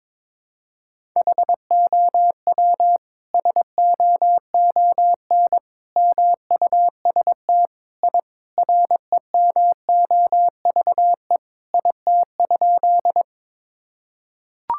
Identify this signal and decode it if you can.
Morse code, heard as a keyed tone, about 21 wpm, 700 Hz. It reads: HOW SOON MUST I REMOVE IT?
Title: Sentences from Top 1000 Words - Review Part 5 - 22wpm